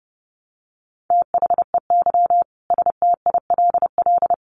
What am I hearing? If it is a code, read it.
Morse code, heard as a keyed tone, about 30 wpm, 700 Hz. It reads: THEY STILL
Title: Sentences from Top 1200 Words 30wpm